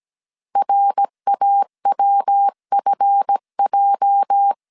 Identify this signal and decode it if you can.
Morse code, heard as a keyed tone, about 17 wpm, 800 Hz. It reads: RAWFJ
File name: CW / DET / 1 9 1500C S C 16wpm 2909